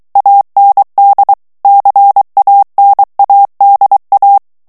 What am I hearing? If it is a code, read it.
Morse code, heard as a keyed tone, about 23 wpm, 800 Hz. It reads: AND CANADA
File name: morse mp3 / ANGLAIS texte clair 2 vitesse 22 mots minute